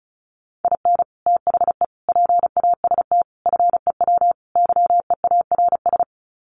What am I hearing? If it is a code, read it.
Morse code, heard as a keyed tone, about 34 wpm, 700 Hz. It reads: IN THE PAST FEW YEARS
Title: Sets of 5 Words 35wpm